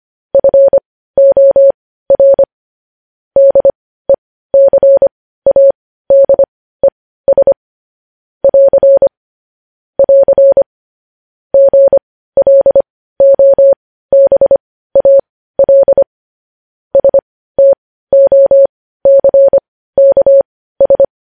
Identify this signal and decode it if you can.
Morse code, heard as a keyed tone, about 25 wpm, spaced as for 15 wpm, 550 Hz. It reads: FOR DECADES <AR> <AR> GLOBAL STOCKS
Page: news headlines in morse code at 15 wpm